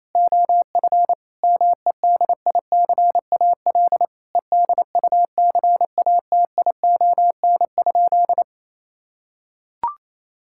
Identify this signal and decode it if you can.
Morse code, heard as a keyed tone, about 28 wpm, 700 Hz. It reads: OF MEDICAL EDUCATION?
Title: Sentences from Top 1000 Words - Review Part 6 - 28wpm